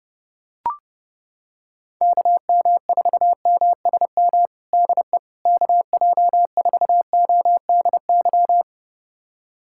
Morse code, heard as a keyed tone, about 32 wpm, 700 Hz. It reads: KM4MSM DE KJ4ODY
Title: USA Callsign DE Callsign 30wpm